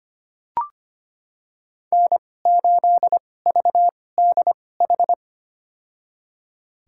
Morse code, heard as a keyed tone, about 25 wpm, spaced as for 18 wpm, 700 Hz. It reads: N8VDH
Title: USA Call Signs - 5 Characters - 2x Character Spacing - 25wpm